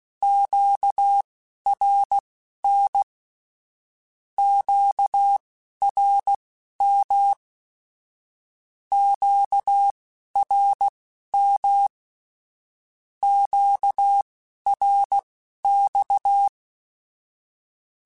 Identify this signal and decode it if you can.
Morse code, heard as a keyed tone, about 16 wpm, spaced as for 11 wpm, 800 Hz. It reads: QRN QRM QRM QRX
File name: (spaced) / Q-Calls Spaced-mp3s / Q Calls Twice sp @15